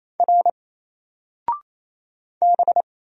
Morse code, heard as a keyed tone, about 27 wpm, 700 Hz. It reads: R B
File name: Single-Letter-Number-28wpm